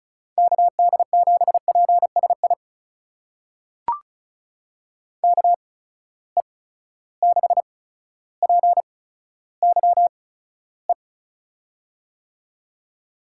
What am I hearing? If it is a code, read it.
Morse code, heard as a keyed tone, about 36 wpm, spaced as for 15 wpm, 700 Hz. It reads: KD7PSI KE6PYE
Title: USA Call Signs - 6 Characters - 8x Character Spacing - 35wpm